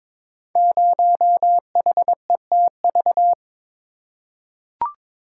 Morse code, heard as a keyed tone, about 23 wpm, 700 Hz. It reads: 0HETV
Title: Code Groups - 5 Letter:Numbers - 22wpm